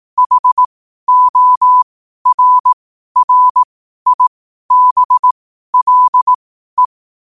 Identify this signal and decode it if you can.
Morse code, heard as a keyed tone, about 18 wpm, spaced as for 12 wpm, 1000 Hz. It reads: HORRIBLE
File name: qotdfiles / 12wpm